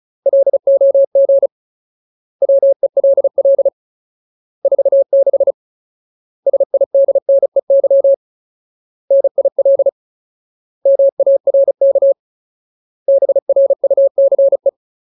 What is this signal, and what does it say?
Morse code, heard as a keyed tone, about 34 wpm, 550 Hz. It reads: LOG WELL 46 SIDNEY NIL MARK BRUCE
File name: Ultimate Word List / 35WPM Ultimate